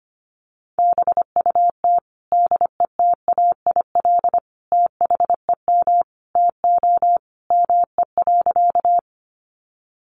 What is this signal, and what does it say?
Morse code, heard as a keyed tone, about 25 wpm, 700 Hz. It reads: BUT DETAIL THEM TO ME.
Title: Sentences from Top 1500 Words 25wpm